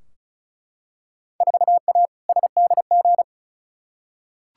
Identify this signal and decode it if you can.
Morse code, heard as a keyed tone, about 36 wpm, 700 Hz. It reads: 4A SDG
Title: Field Day Exchanges - 35wpm